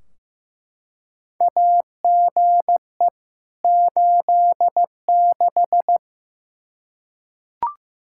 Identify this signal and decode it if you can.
Morse code, heard as a keyed tone, about 15 wpm, 700 Hz. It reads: AGE 86
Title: K1USN SST QSO Practice 15wpm